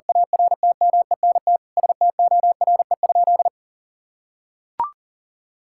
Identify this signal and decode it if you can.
Morse code, heard as a keyed tone, about 40 wpm, 700 Hz. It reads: ARTMENT STORE?
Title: Sentences from Top 1000 Words - Review Part 5 - 40wpm